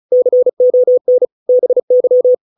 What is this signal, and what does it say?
Morse code, heard as a keyed tone, about 34 wpm, 500 Hz. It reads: CON BY
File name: 632 words / 632 Words 35WPM